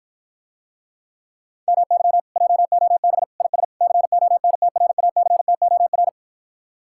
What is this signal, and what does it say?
Morse code, heard as a keyed tone, about 52 wpm, 700 Hz. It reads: MY JOB IS CONTRACTOR